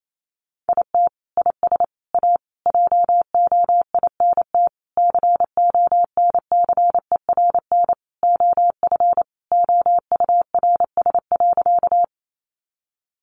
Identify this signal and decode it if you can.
Morse code, heard as a keyed tone, about 28 wpm, 700 Hz. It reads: IT IS A JOINT CONCERN OF OURS.
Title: Sentences from Top 1300 Words 28wpm